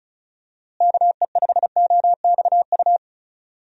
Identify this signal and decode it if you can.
Morse code, heard as a keyed tone, about 36 wpm, 700 Hz. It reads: KE5OXU
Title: USA Call Signs - 6 Characters - Repeated 3x - 35wpm